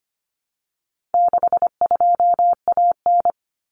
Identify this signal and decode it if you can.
Morse code, heard as a keyed tone, about 25 wpm, 700 Hz. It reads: 62AN
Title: Common Cut Numbers - 4-Digit 25wpm